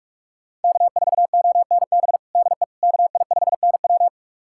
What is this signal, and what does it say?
Morse code, heard as a keyed tone, about 46 wpm, 690 Hz. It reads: K4ONB DE KI5NW
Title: USA Callsign DE Callsign 45wpm